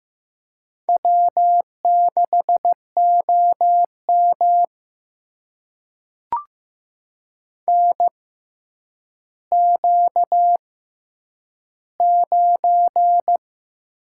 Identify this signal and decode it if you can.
Morse code, heard as a keyed tone, about 15 wpm, spaced as for 9 wpm, 700 Hz. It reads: W6OM NQ9
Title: USA Call Signs - 4 Characters - 6x Character Spacing - 15wpm